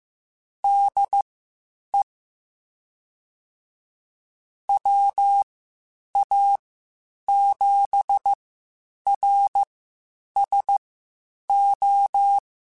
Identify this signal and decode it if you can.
Morse code, heard as a keyed tone, about 15 wpm, spaced as for 8 wpm, 800 Hz. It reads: DE WA7RSO
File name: (spaced) / QSOs Spaced-mp3s / WD0FBT 1 @13 sp